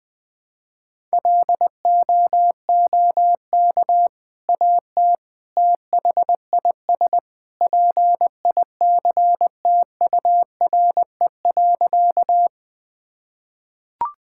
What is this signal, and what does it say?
Morse code, heard as a keyed tone, about 20 wpm, 700 Hz. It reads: LOOK AT THIS PICTURE.
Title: Sentences with 2-Word Phrases - 20wpm